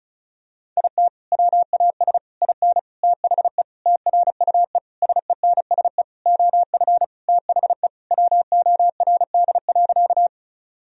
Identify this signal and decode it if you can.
Morse code, heard as a keyed tone, about 36 wpm, 700 Hz. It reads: IT WAS IN THE TRUE SENSE OF THE WORD.
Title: Sentences from Top 1000 Words - Review Part 4 - 35wpm